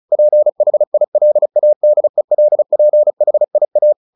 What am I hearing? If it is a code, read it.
Morse code, heard as a keyed tone, about 34 wpm, 600 Hz. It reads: PHILADELPHIA
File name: the old man and the sea / the old man and the sea 1 35wpm